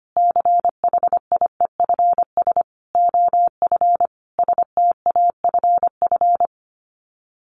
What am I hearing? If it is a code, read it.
Morse code, heard as a keyed tone, about 25 wpm, 700 Hz. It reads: CHIEFS OF STAFF
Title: Sets of 5 Words Encore2 25wpm